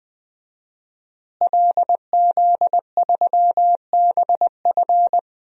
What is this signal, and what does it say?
Morse code, heard as a keyed tone, about 20 wpm, 700 Hz. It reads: LZ3BF
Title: DXCC-100 Call Signs - 5 Characters - Repeated 3x - 20wpm